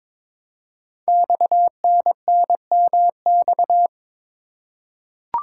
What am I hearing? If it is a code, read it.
Morse code, heard as a keyed tone, about 23 wpm, 700 Hz. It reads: XNNMX
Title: Code Groups - 5 Letters - 22wpm